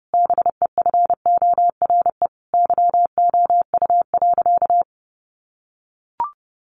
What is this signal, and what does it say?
Morse code, heard as a keyed tone, about 30 wpm, 700 Hz. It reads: BEFORE YOU.
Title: Sentences from Top 1400 Words 30wpm